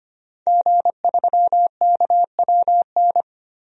Morse code, heard as a keyed tone, about 25 wpm, 700 Hz. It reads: G3KWN
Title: DXCC-100 Call Signs - 5 Characters - 25wpm